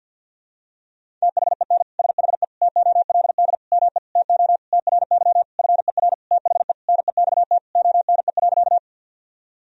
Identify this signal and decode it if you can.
Morse code, heard as a keyed tone, about 51 wpm, 700 Hz. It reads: THEN SHE TOLD ME TO TRY FER THE NEXT ONE.
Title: Sentences from Top 400 Words and Abbreviations 50wpm